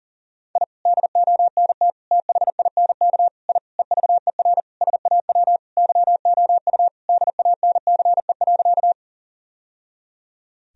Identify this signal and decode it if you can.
Morse code, heard as a keyed tone, about 40 wpm, 700 Hz. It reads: I DONT THINK I EVER SAW YOU DANCE.
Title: Sentences from Top 1000 Words - Review Part 4 - 40wpm